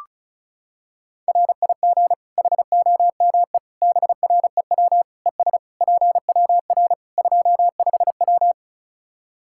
Morse code, heard as a keyed tone, about 34 wpm, 700 Hz. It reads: RIG HOME BREW ES PWR 25W
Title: Practice QSO - Exchange 2 - Info - 35wpm